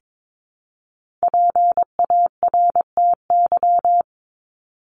Morse code, heard as a keyed tone, about 22 wpm, 700 Hz. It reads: PARTY
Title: ES:S Suffix - Top 100 Words with Base Words 22wpm